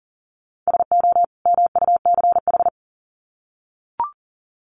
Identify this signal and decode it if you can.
Morse code, heard as a keyed tone, about 40 wpm, 700 Hz. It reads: SO MUCH
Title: Sets of 2 Words 40wpm